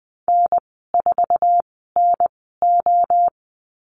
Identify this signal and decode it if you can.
Morse code, heard as a keyed tone, about 20 wpm, 700 Hz. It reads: N 4 N O
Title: 062 Lesson 11g - Letter h - Call Signs 20wpm